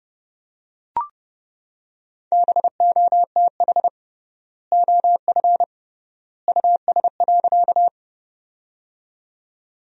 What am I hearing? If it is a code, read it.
Morse code, heard as a keyed tone, about 30 wpm, 700 Hz. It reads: BOTH OF US.
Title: 3x Word-Spacing - Sentences from Top 300 Words - 30wpm